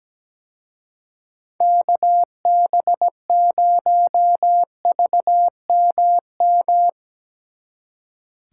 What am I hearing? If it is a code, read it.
Morse code, heard as a keyed tone, about 19 wpm, 700 Hz. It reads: KB0VMM
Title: Speed Racing - USA Call Signs 17wpm